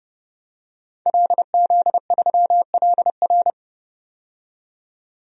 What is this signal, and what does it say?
Morse code, heard as a keyed tone, about 29 wpm, 700 Hz. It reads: LZ3LR
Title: DXCC-100 Call Signs - 5 Characters - 30wpm